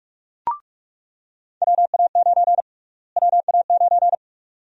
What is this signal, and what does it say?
Morse code, heard as a keyed tone, about 48 wpm, 700 Hz. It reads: WA9 WA9
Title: USA Call Sign Prefixes - Repeated 2x - 45wpm